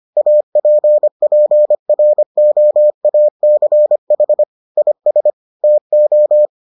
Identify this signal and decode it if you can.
Morse code, heard as a keyed tone, about 25 wpm, 600 Hz. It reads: APPROACH IS TO